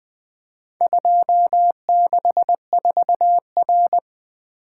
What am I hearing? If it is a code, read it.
Morse code, heard as a keyed tone, about 23 wpm, 700 Hz. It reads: 264R
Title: Speed-Racing - Four Letter-Number Combinations - 20wpm